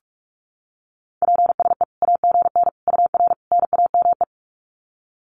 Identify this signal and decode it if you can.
Morse code, heard as a keyed tone, about 44 wpm, 700 Hz. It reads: PSE AGN UR NAME
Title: Common QSO Phrases 45wpm